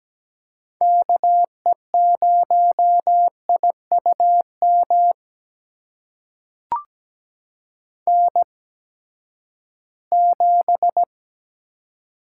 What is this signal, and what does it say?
Morse code, heard as a keyed tone, about 17 wpm, spaced as for 12 wpm, 700 Hz. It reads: KE0IUM N7
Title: USA Call Signs - 8x Character Spacing - 17wpm